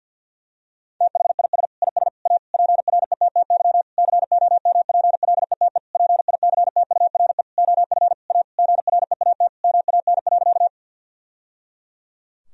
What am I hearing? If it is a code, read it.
Morse code, heard as a keyed tone, about 52 wpm, 700 Hz. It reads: THIS IS A PRETTY COMPLETE PICTURE OF A GREAT MAN.
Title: Sentences from Top 1000 Words - Review Part 2 - 50wpm